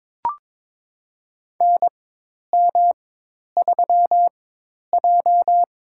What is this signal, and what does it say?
Morse code, heard as a keyed tone, about 22 wpm, spaced as for 10 wpm, 700 Hz. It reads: NM3J
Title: USA Call Signs - 4 Characters - 4x Character Spacing - 22wpm